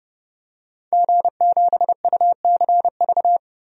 Morse code, heard as a keyed tone, about 30 wpm, 700 Hz. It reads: G7UCV
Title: DXCC-100 Call Signs - 5 Characters - Repeated 2x - 30wpm